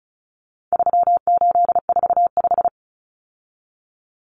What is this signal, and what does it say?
Morse code, heard as a keyed tone, about 34 wpm, 700 Hz. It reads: 3845
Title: Common Cut Numbers - 4-Digit 35wpm